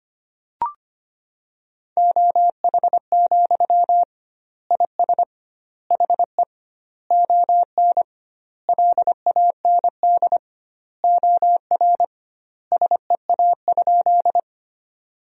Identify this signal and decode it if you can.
Morse code, heard as a keyed tone, about 25 wpm, 700 Hz. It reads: OH, IS HE ON LAND OR SEA?
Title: 2x Word-Spacing - Sentences from Top 500 Words - 25wpm